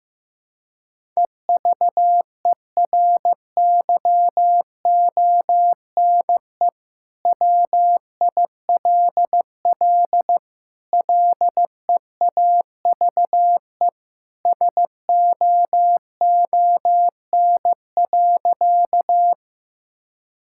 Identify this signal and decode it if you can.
Morse code, heard as a keyed tone, about 16 wpm, 700 Hz. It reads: EVERYONE WILL LEAVE SOON.
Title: Sentences from Top 1000 Words - Review Part 5 - 15wpm